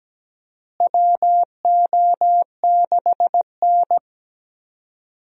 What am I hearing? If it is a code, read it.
Morse code, heard as a keyed tone, about 17 wpm, 700 Hz. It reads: WO6N
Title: USA Call Signs - 4 Characters - Repeated 2x - 17wpm